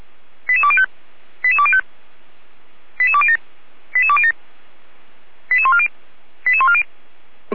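Five-tone selective call (ZVEI), heard as quick runs of five tones, tones of about 70 ms. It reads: 88205 88205 88207 88207 88140 88140
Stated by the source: ZVEI five-tone selective call received on SDR.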